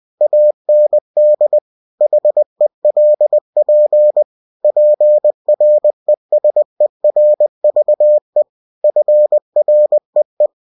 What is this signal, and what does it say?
Morse code, heard as a keyed tone, about 20 wpm, 600 Hz. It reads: AND HELP PRESERVE FREE
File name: cw-books / winnie-the-pooh / winnie-20wpm-page-0113